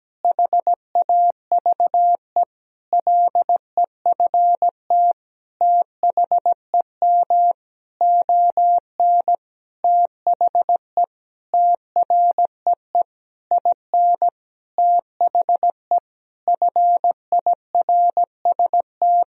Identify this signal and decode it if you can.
Morse code, heard as a keyed tone, about 17 wpm, 700 Hz. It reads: HAVE LEFT THEM ON THE TREE IN THE FIRST
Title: Sentences from Top 1000 Words - Review Part 1 - 17wpm